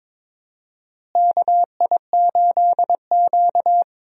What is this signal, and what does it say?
Morse code, heard as a keyed tone, about 22 wpm, 700 Hz. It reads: KI8Q
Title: USA Call Signs - 4 Characters - Repeated 2x - 22wpm